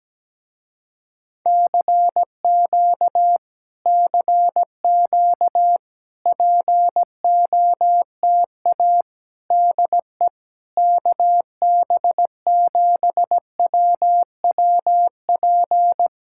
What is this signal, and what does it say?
Morse code, heard as a keyed tone, about 17 wpm, 700 Hz. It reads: CQ CQ POTA DE KB7WWP
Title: Speed-Racing Common QSO Phrases - 17wpm